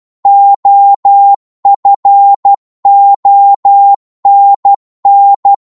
Morse code, heard as a keyed tone, about 12 wpm, 800 Hz. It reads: OFONN
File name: Morse / ltr-12wpm